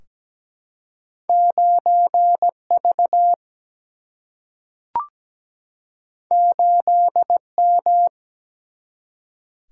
Morse code, heard as a keyed tone, about 17 wpm, 700 Hz. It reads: 9V 8M
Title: Single-Letter Single-Number 17wpm